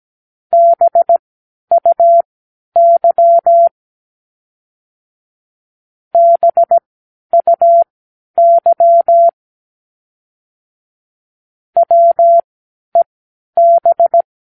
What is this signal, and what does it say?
Morse code, heard as a keyed tone, about 17 wpm, spaced as for 10 wpm, 680 Hz. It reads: BUY BUY WEB